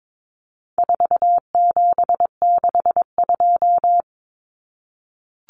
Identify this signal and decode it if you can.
Morse code, heard as a keyed tone, about 22 wpm, 700 Hz. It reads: 4762
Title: Numbers - 4-Digit 22wpm